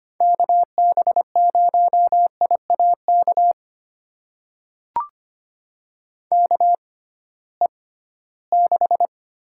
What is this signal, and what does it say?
Morse code, heard as a keyed tone, about 25 wpm, spaced as for 16 wpm, 700 Hz. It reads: KB0IAK KE6
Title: USA Call Signs - 6 Characters - 6x Character Spacing - 25wpm